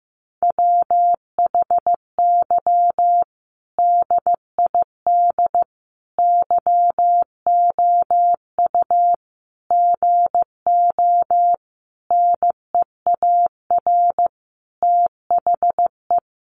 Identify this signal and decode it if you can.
Morse code, heard as a keyed tone, about 15 wpm, 700 Hz. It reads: WHY DID YOU GO NEAR THE
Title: Sentences from Top 1100 Words 15wpm